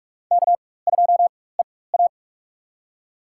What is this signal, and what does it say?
Morse code, heard as a keyed tone, about 43 wpm, spaced as for 21 wpm, 700 Hz. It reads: K2EA
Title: USA Call Signs - 4x Character Spacing - 45wpm